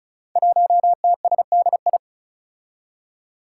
Morse code, heard as a keyed tone, about 36 wpm, 700 Hz. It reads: 1TSDI
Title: Five Challenging Letter-Number Combinations 35wpm